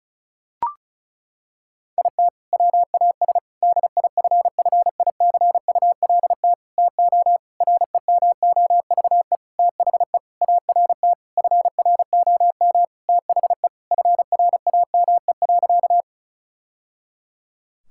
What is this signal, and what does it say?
Morse code, heard as a keyed tone, about 36 wpm, 700 Hz. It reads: IT WAS DIFFICULT TO REMOVE THE ART FROM THE FRAME.
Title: Sentences from Top 1000 Words - Review Part 6 - 35wpm